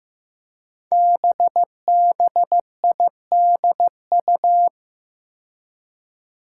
Morse code, heard as a keyed tone, about 15 wpm, 700 Hz. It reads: BBIDU
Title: Five Challenging Letter-Number Combinations 15wpm